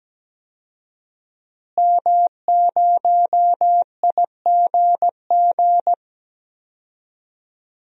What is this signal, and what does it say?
Morse code, heard as a keyed tone, about 17 wpm, 700 Hz. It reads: M0IGG
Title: DXCC-100 Call Signs - 5 Characters - Repeated 2x - 17wpm